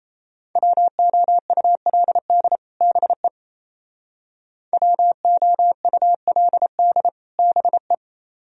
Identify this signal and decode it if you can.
Morse code, heard as a keyed tone, about 34 wpm, 700 Hz. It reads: WOULD BE WOULD BE
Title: Speed-Racing - Sets of 2 Words 28wpm